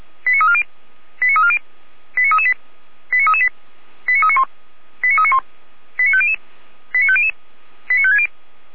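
Five-tone selective call (ZVEI), heard as quick runs of five tones, tones of about 70 ms. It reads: 89340 89340 89338 89338 89391 89391 89500 89500 89560
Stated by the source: ZVEI five-tone selective call received by SDR at fire brigade Orlamünde (feuerwehr-orlamuende).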